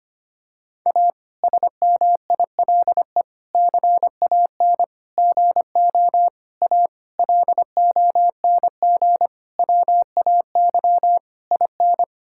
Morse code, heard as a keyed tone, about 25 wpm, 700 Hz. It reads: A SMILE CAN GO A LONG WAY IN
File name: practice / 25wpm / Difficult Sentences with Hints - 25wpm